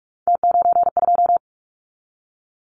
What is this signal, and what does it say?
Morse code, heard as a keyed tone, about 44 wpm, 700 Hz. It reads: T92
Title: Callsign Prefixes for Top 100 DXCC - 45wpm